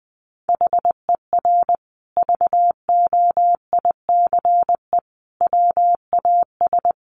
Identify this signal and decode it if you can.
Morse code, heard as a keyed tone, about 20 wpm, 700 Hz. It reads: HER VOICE WAS
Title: Sentences from Top 500 Words 20wpm